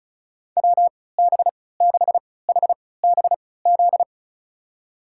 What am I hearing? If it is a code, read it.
Morse code, heard as a keyed tone, about 36 wpm, spaced as for 20 wpm, 700 Hz. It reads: WB6HBZ
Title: USA Call Signs - 6 Characters - 3x Character Spacing - 35wpm